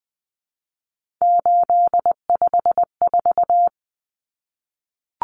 Morse code, heard as a keyed tone, about 20 wpm, 700 Hz. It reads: 854
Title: Common Cut Numbers - 3-Digit 20wpm